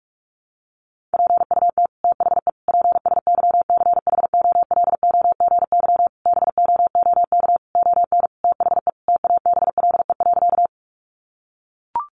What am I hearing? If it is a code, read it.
Morse code, heard as a keyed tone, about 44 wpm, 700 Hz. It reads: PUT THE PSYCHOLOGY BOOK ON THE TABLE.
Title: Sentences from Top 1400 Words 45wpm